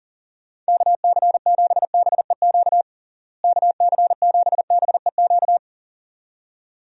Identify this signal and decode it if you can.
Morse code, heard as a keyed tone, about 40 wpm, 700 Hz. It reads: KC7BEQ KC7BEQ
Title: USA Call Signs - 6 Characters - Repeated 2x - 40wpm